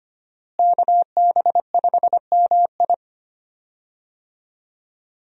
Morse code, heard as a keyed tone, about 25 wpm, 700 Hz. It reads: KB5MI
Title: USA Call Signs - 5 Characters - Repeated 3x - 25wpm